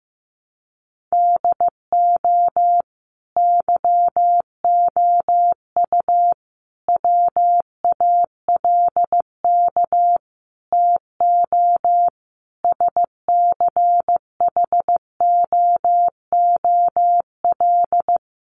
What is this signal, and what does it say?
Morse code, heard as a keyed tone, about 14 wpm, 700 Hz. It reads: DO YOU WALK TO SCHOOL
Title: Sentences from Top 1400 Words 15wpm